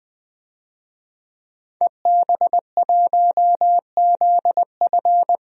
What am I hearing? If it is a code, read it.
Morse code, heard as a keyed tone, about 20 wpm, 700 Hz. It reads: EB1ZF
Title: DXCC-100 Call Signs - 5 Characters - 20wpm